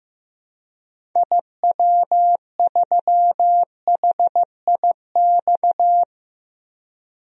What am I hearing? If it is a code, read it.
Morse code, heard as a keyed tone, about 16 wpm, 700 Hz. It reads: IW3HIX
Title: DXCC-100 Call Signs - 6 Characters - 15wpm